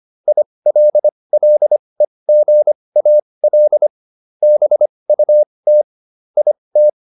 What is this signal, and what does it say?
Morse code, heard as a keyed tone, about 25 wpm, spaced as for 20 wpm, 600 Hz. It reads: ILLEGAL BUT IT